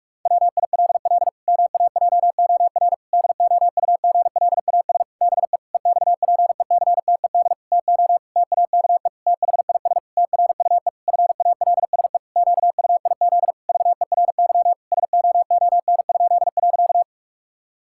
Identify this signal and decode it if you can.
Morse code, heard as a keyed tone, about 46 wpm, 700 Hz. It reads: WILL MAJOR DOUGLAS BE EXPECTED TO TAKE THIS TRUE FALSE QUIZ VERY SOON?.
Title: Pangram Challenge v2 - 45wpm